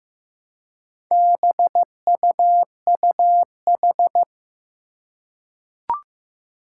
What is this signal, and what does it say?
Morse code, heard as a keyed tone, about 15 wpm, 700 Hz. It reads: BUUH
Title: Four Challenging Letter-Number Combinations 15wpm